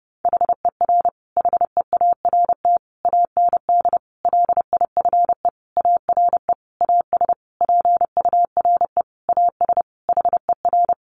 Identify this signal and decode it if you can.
Morse code, heard as a keyed tone, about 30 wpm, 700 Hz. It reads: HER HEART AND LIFE ARE AS PURE AS HER
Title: Sentences from Top 1500 Words 30wpm